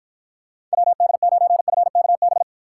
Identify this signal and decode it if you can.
Morse code, heard as a keyed tone, about 54 wpm, 700 Hz. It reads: WD9FKB